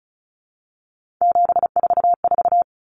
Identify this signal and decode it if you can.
Morse code, heard as a keyed tone, about 34 wpm, 700 Hz. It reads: 744
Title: Numbers - 3-Digit 35wpm